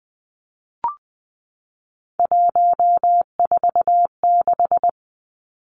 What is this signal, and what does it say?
Morse code, heard as a keyed tone, about 20 wpm, 700 Hz. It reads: 146
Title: Numbers - 3-Digit 20wpm